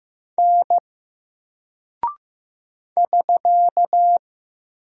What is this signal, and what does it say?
Morse code, heard as a keyed tone, about 15 wpm, 700 Hz. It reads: N <SK>
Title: Single Character (Letter:Number:Punctuation:Prosign) 15wpm